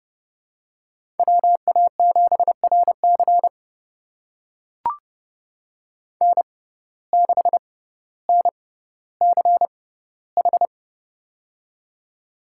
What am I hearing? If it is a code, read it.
Morse code, heard as a keyed tone, about 30 wpm, spaced as for 15 wpm, 700 Hz. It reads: WA7RC N6NCH
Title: USA Call Signs - 5 Characters - 6x Character Spacing - 30wpm